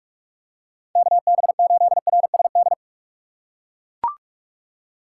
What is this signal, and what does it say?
Morse code, heard as a keyed tone, about 44 wpm, 700 Hz. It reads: KB8RSD
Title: USA Call Signs - 6 Characters - 6x Character Spacing - 45wpm